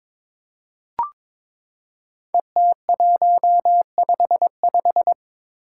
Morse code, heard as a keyed tone, about 22 wpm, 700 Hz. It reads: ET155
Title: Five Challenging Letter-Number Combinations 22wpm